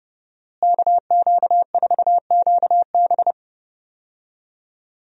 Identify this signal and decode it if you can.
Morse code, heard as a keyed tone, about 30 wpm, 700 Hz. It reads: KQ4QB
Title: USA Call Signs - 5 Characters - Repeated 2x - 30wpm